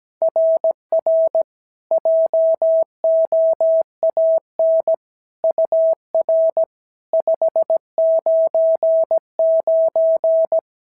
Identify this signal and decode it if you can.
Morse code, heard as a keyed tone, about 17 wpm, 650 Hz. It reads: RR JOAN UR 599
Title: SOTA QSO Practice 17wpm